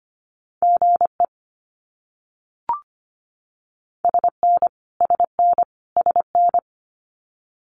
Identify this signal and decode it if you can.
Morse code, heard as a keyed tone, about 25 wpm, 700 Hz. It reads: GE SN SN SN
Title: Abbreviations Repeated 3x - 25wpm